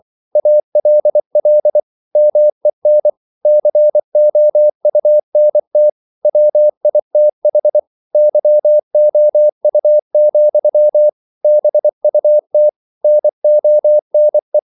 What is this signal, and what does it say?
Morse code, heard as a keyed tone, about 24 wpm, 600 Hz. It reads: ALL MEN COUNT WITH YOU, BUT NONE
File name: CW / Practice / If24